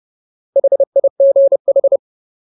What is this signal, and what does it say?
Morse code, heard as a keyed tone, about 30 wpm, 550 Hz. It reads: HIGH
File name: Ultimate Word List / 30WPM Ultimate